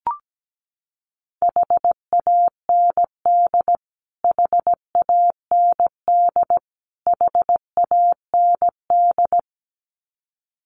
Morse code, heard as a keyed tone, about 17 wpm, 700 Hz. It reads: HAND HAND HAND
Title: Top 500 Words Repeated 17wpm